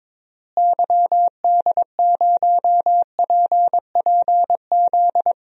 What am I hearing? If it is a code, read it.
Morse code, heard as a keyed tone, about 22 wpm, 700 Hz. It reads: YD0PPZ
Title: DXCC-100 Call Signs - 6 Characters - Repeated 3x - 22wpm